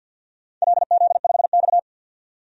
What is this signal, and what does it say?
Morse code, heard as a keyed tone, about 48 wpm, 700 Hz. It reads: LZ5X
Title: DXCC-100 Call Signs - 4 Characters - 50wpm